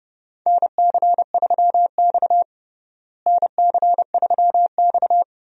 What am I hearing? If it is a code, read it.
Morse code, heard as a keyed tone, about 30 wpm, 700 Hz. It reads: NC3X NC3X
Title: USA Call Signs - 4 Characters - Repeated 2x - 30wpm